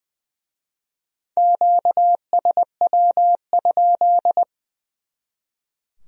Morse code, heard as a keyed tone, about 23 wpm, 700 Hz. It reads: QSW?
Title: Speed-Racing Q-Codes - 20wpm